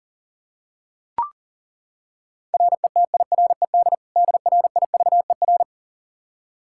Morse code, heard as a keyed tone, about 37 wpm, 700 Hz. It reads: RETIRED DRIVER
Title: Practice QSO - Exchange 3 - More Info - 40wpm